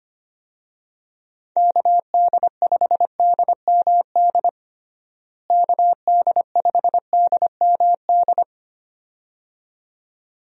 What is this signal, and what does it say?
Morse code, heard as a keyed tone, about 25 wpm, 700 Hz. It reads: KD5DMD KD5DMD
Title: USA Call Signs - 6 Characters - Repeated 2x - 25wpm